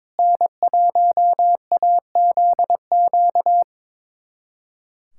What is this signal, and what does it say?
Morse code, heard as a keyed tone, about 22 wpm, 700 Hz. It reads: N1AZQ
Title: USA Callsign DE Callsign - 5 Characters - 22wpm